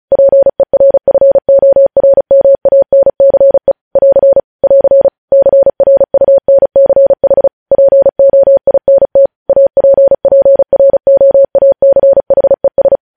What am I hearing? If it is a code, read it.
Morse code, heard as a keyed tone, about 36 wpm, 550 Hz. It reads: PERFORMANCE <AR> <AR> CRUNCH POINT APPROACHES